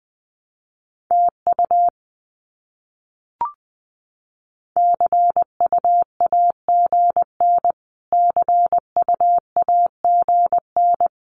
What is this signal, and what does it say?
Morse code, heard as a keyed tone, about 19 wpm, 700 Hz. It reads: TU CUAGN CUAGN
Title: Abbreviations Repeated 3x - 20wpm